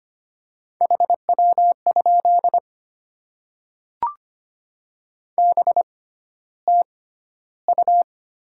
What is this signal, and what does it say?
Morse code, heard as a keyed tone, about 25 wpm, spaced as for 13 wpm, 700 Hz. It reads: HW? BTU
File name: Top 20 QSO Elements - 6x Character Spacing - 25wpm